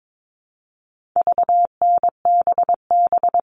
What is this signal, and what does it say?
Morse code, heard as a keyed tone, about 22 wpm, 700 Hz. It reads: VNBB
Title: All Cut Numbers - 4-Digit 22wpm